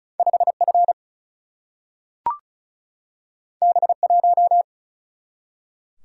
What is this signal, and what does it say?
Morse code, heard as a keyed tone, about 36 wpm, 700 Hz. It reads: 5F B1
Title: Single-Letter Single-Number 35wpm